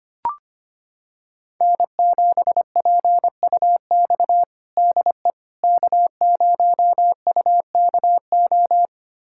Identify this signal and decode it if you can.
Morse code, heard as a keyed tone, about 25 wpm, 700 Hz. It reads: N7PUX DE K0UKO
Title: USA Callsign DE Callsign - 5 Characters - 25wpm